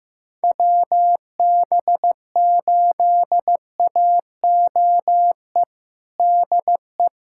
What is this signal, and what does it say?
Morse code, heard as a keyed tone, about 16 wpm, 700 Hz. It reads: WB8AOE DE
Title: USA Callsign DE Callsign - 6 Characters - 15wpm